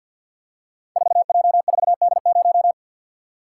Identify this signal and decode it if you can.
Morse code, heard as a keyed tone, about 52 wpm, 700 Hz. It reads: 4J4D0
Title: Code Groups - 5 Letter:Numbers - 50wpm